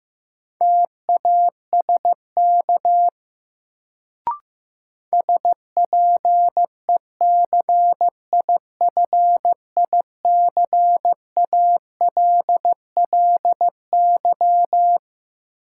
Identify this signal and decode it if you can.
Morse code, heard as a keyed tone, about 15 wpm, 700 Hz. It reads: TASK SPECIFICALLY
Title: Sentences from Top 1100 Words - New Word Drill Alt - 15wpm